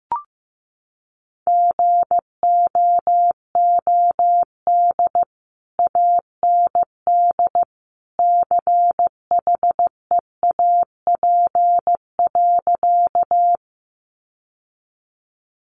Morse code, heard as a keyed tone, about 15 wpm, 700 Hz. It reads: GOOD AND CHEAP.
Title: Sentences from Top 1200 Words 15wpm